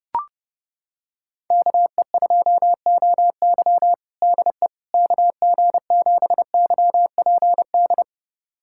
Morse code, heard as a keyed tone, about 32 wpm, 690 Hz. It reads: KE2OY DE KG7YPD
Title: USA Callsign DE Callsign 30wpm